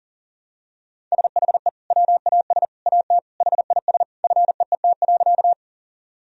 Morse code, heard as a keyed tone, about 40 wpm, 700 Hz. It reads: SHE WAS AT HIS FEET.
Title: Sentences from Top 1000 Words - Review Part 1 - 40wpm